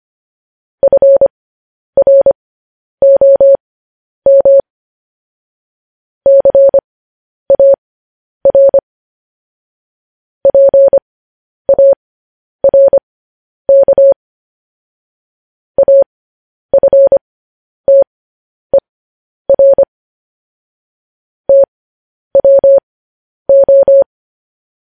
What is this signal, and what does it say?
Morse code, heard as a keyed tone, about 25 wpm, spaced as for 10 wpm, 550 Hz. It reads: FROM CAR PARK AFTER TWO
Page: news headlines in morse code at 10 wpm